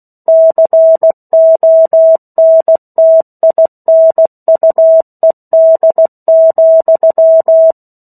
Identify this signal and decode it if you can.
Morse code, heard as a keyed tone, about 16 wpm, 650 Hz. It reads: CONTINUED,
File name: 03-Chapter-03 16 WPM